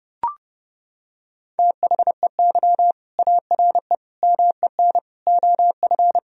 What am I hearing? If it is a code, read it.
Morse code, heard as a keyed tone, about 30 wpm, 700 Hz. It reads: THEY ARE MEN OF
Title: Sentences from Top 1000 Words - Review Part 2 - 30wpm